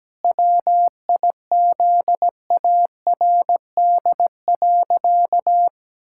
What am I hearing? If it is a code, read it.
Morse code, heard as a keyed tone, about 17 wpm, 700 Hz. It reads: WIZARD.
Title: Pangram Challenge v2 - 17wpm